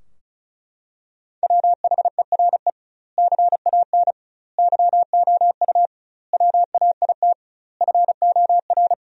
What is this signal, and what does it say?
Morse code, heard as a keyed tone, about 36 wpm, 700 Hz. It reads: WHERE CAN YOU WAIT FOR
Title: 2x Word-Spacing - Sentences from Top 500 Words - 35wpm